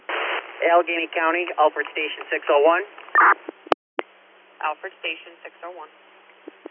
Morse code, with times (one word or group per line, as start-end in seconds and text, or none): none